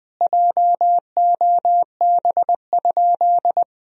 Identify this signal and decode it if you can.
Morse code, heard as a keyed tone, about 20 wpm, 700 Hz. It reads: JOB?
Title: Sentences from Top 1000 Words - Review Part 3 - 20wpm